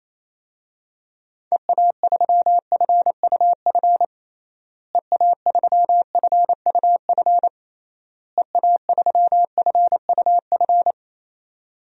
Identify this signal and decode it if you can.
Morse code, heard as a keyed tone, about 27 wpm, 700 Hz. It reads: EA3FUF EA3FUF EA3FUF
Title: DXCC-100 Call Signs - 6 Characters - Repeated 3x - 28wpm